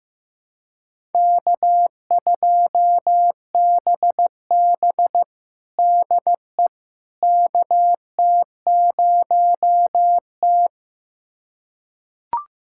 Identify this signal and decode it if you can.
Morse code, heard as a keyed tone, about 15 wpm, 700 Hz. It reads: K2BB DE KT0T
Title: USA Callsign DE Callsign - 4 Characters - 15wpm